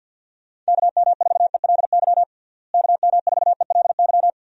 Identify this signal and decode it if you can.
Morse code, heard as a keyed tone, about 51 wpm, 700 Hz. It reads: KM4ELY KM4ELY
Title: USA Call Signs - 6 Characters - Repeated 2x - 50wpm